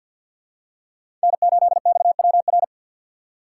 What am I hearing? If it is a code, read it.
Morse code, heard as a keyed tone, about 49 wpm, 700 Hz. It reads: N8XWR
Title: USA Call Signs - 5 Characters - 4x Character Spacing - 50wpm